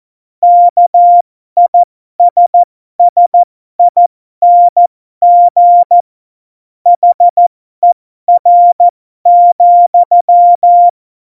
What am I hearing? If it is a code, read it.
Morse code, heard as a keyed tone, about 14 wpm, spaced as for 12 wpm, 700 Hz. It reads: KISSING HER,
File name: Morsekurs / Witze